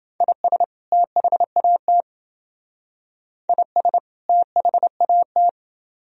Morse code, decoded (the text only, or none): IS THAT IS THAT